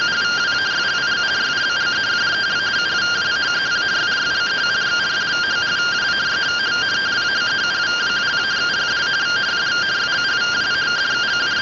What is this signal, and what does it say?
Two-tone teleprinter signal, about 170 Hz shift.